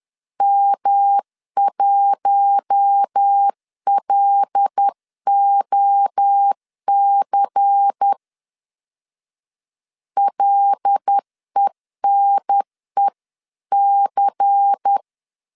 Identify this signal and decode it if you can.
Morse code, heard as a keyed tone, about 10 wpm, 800 Hz. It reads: M1LOC LENEC
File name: CW / DET / 1 6 1500C S C 10wpm 4252